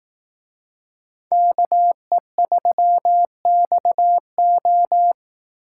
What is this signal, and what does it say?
Morse code, heard as a keyed tone, about 18 wpm, 700 Hz. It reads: KE3XO
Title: Speed Racing - USA 2x2 Call Signs 15wpm